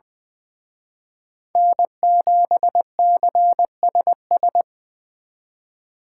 Morse code, heard as a keyed tone, about 24 wpm, 700 Hz. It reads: N7CSS
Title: Speed Racing - USA 1x3 Call Signs 20wpm